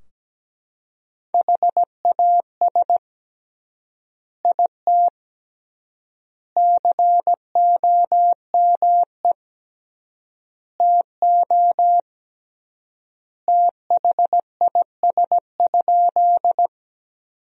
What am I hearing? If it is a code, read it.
Morse code, heard as a keyed tone, about 17 wpm, 700 Hz. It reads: HAS IT COME TO THIS?
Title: 3x Word-Spacing - Sentences from Top 100 Words - 17wpm